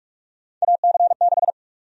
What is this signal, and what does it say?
Morse code, heard as a keyed tone, about 46 wpm, 700 Hz. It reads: AC6
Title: USA Call Sign Prefixes - 45wpm